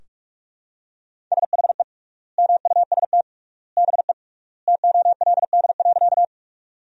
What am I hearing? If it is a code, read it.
Morse code, heard as a keyed tone, about 46 wpm, 700 Hz. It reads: SHE MUST BE TOLD.
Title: 3x Word-Spacing - Sentences from Top 400 Words - 45wpm